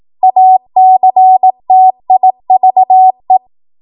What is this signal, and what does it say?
Morse code, heard as a keyed tone, about 18 wpm, 750 Hz. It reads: ACTIVE
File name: Morse / 18WPM